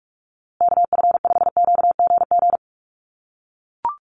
Morse code, heard as a keyed tone, about 44 wpm, 700 Hz. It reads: KF5QGG
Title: USA Call Signs Repeated 2x 45wpm